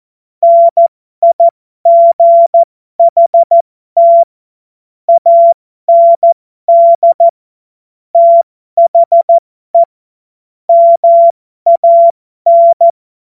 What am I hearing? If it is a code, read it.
Morse code, heard as a keyed tone, about 14 wpm, spaced as for 12 wpm, 680 Hz. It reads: NIGHT AND THE MAN
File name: Morsekurs / Witze